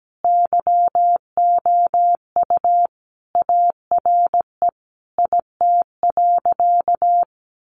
Morse code, heard as a keyed tone, about 17 wpm, 700 Hz. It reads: YOU ARE IT.
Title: Additional Easy Phrases 17wpm